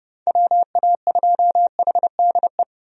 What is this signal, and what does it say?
Morse code, heard as a keyed tone, about 32 wpm, 700 Hz. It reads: WA2HDE
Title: USA Call Signs - 6 Characters - 30wpm